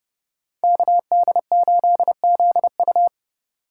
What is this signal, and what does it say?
Morse code, heard as a keyed tone, about 29 wpm, 700 Hz. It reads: KD8ZU